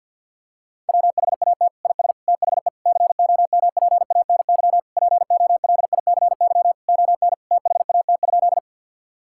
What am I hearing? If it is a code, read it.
Morse code, heard as a keyed tone, about 52 wpm, 700 Hz. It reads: WHAT IS THE COMPANY POLICY ON THAT?
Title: Sentences from Top 1000 Words - Review Part 5 - 50wpm